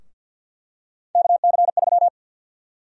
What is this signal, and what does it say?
Morse code, heard as a keyed tone, about 52 wpm, 700 Hz. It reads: KC3
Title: USA Call Sign Prefixes - Repeated 2x - 50wpm